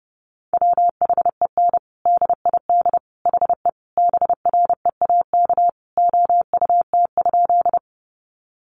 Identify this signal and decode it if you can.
Morse code, heard as a keyed tone, about 30 wpm, 700 Hz. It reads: WHEN DID HE BREAK OUT?
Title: Sentences from Top 900 Words 30wpm